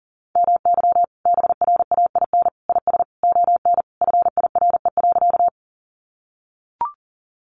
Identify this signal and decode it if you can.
Morse code, heard as a keyed tone, about 37 wpm, 700 Hz. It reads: MY BRAIN IS ON FIRE.
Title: Sentences from Top 1100 Words 40wpm